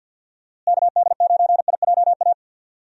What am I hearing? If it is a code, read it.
Morse code, heard as a keyed tone, about 49 wpm, 700 Hz. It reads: KD9IJA
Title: USA Call Signs - 6 Characters - Repeated 3x - 50wpm